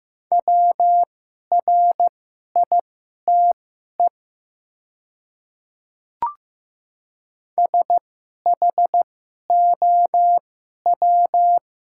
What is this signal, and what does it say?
Morse code, heard as a keyed tone, about 15 wpm, spaced as for 11 wpm, 700 Hz. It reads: WRITE SHOW
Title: 2x Character Spacing - Top 200 Words - 15wpm